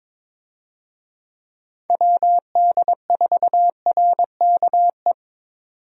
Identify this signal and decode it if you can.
Morse code, heard as a keyed tone, about 22 wpm, 700 Hz. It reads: WD4RKE
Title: USA Call Signs - 6 Characters - 22wpm